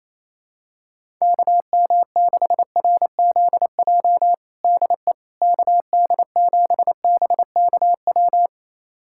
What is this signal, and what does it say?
Morse code, heard as a keyed tone, about 27 wpm, 700 Hz. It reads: KM6RZJ DE KD7BKW
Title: USA Callsign DE Callsign - 6 Characters - 28wpm